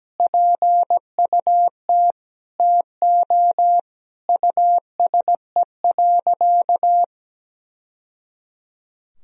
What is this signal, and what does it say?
Morse code, heard as a keyed tone, about 17 wpm, 700 Hz. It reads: PUT TO USE.
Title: Sentences from Top 1000 Words - Review Part 3 - 17wpm